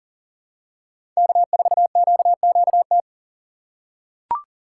Morse code, heard as a keyed tone, about 38 wpm, 700 Hz. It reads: K4QQT
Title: USA Call Signs - 5 Characters - 3x Character Spacing - 40wpm